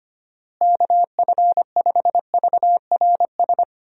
Morse code, heard as a keyed tone, about 25 wpm, 700 Hz. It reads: KF5VRS
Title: USA Call Signs - 6 Characters - Repeated 2x - 25wpm